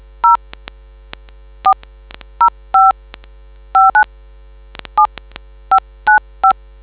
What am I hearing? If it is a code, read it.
DTMF, heard as telephone touch tones, tones of about 110 ms. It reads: 0405597595